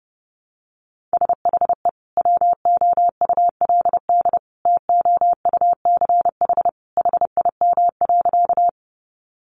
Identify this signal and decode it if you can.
Morse code, heard as a keyed tone, about 30 wpm, 700 Hz. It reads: SHE WOULD TOUCH HIM.
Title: Sentences from Top 1200 Words 30wpm